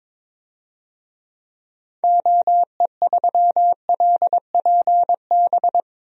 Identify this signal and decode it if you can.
Morse code, heard as a keyed tone, about 21 wpm, 700 Hz. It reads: OE3LPB
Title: DXCC-100 Call Signs - 6 Characters - Repeated 2x - 22wpm